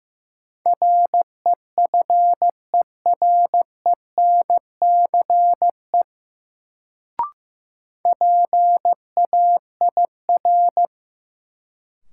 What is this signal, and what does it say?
Morse code, heard as a keyed tone, about 16 wpm, 700 Hz. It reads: REFERENCE PAIR
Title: Sentences from Top 1200 Words - New Word Drill Alt - 15wpm